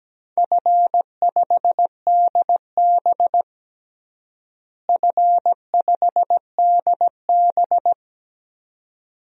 Morse code, heard as a keyed tone, about 17 wpm, 700 Hz. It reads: F5DB F5DB
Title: DXCC-100 Call Signs - 4 Characters - Repeated 2x - 17wpm